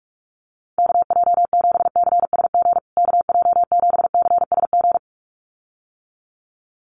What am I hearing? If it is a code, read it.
Morse code, heard as a keyed tone, about 44 wpm, 700 Hz. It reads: KJ7CSG KJ7CSG
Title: USA Call Signs Repeated 2x 45wpm